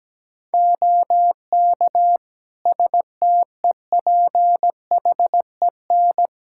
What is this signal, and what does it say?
Morse code, heard as a keyed tone, about 17 wpm, 700 Hz. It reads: OK STEPHEN
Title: Practice QSO - Exchange 2 - Info - 17wpm